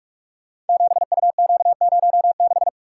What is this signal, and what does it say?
Morse code, heard as a keyed tone, about 45 wpm, 700 Hz. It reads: 7UQ06
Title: Code Groups - 5 Letter:Numbers - 45wpm